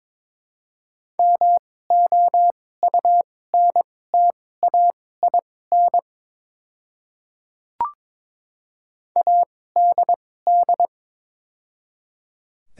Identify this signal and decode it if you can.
Morse code, heard as a keyed tone, about 22 wpm, spaced as for 16 wpm, 700 Hz. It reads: MOUNTAIN ADD
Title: 2x Character Spacing - Top 300 Words - 22wpm